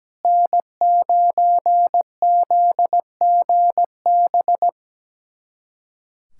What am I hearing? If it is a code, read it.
Morse code, heard as a keyed tone, about 17 wpm, 700 Hz. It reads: N9ZGB
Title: USA Callsign DE Callsign - 5 Characters - 17wpm